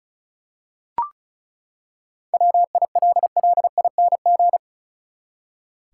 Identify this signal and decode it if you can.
Morse code, heard as a keyed tone, about 36 wpm, 700 Hz. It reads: WILLING
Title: Seven Letter Words - 35wpm